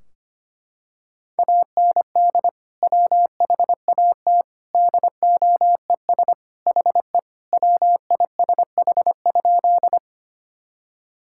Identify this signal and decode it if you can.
Morse code, heard as a keyed tone, about 25 wpm, 700 Hz. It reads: AND WHAT DOES HE WISH?
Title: Sentences from Top 1600 Words 25wpm